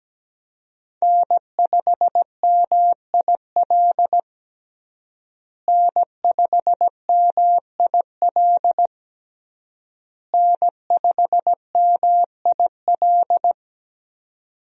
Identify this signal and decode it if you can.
Morse code, heard as a keyed tone, about 17 wpm, 700 Hz. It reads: N5MIL N5MIL N5MIL
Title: USA Call Signs - 5 Characters - Repeated 3x - 17wpm